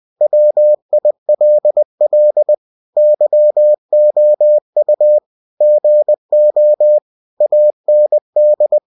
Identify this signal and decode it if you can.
Morse code, heard as a keyed tone, about 20 wpm, 600 Hz. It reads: WILL YOU GO AND
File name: cw-books / winnie-the-pooh / winnie-20wpm-page-0102